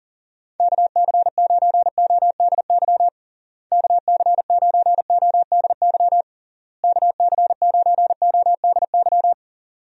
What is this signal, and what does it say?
Morse code, heard as a keyed tone, about 40 wpm, 700 Hz. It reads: KC9ODY KC9ODY KC9ODY
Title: USA Call Signs - 6 Characters - Repeated 3x - 40wpm